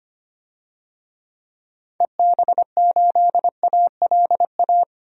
Morse code, heard as a keyed tone, about 25 wpm, 700 Hz. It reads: EB8ALA
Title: DXCC-100 Call Signs - 6 Characters - 25wpm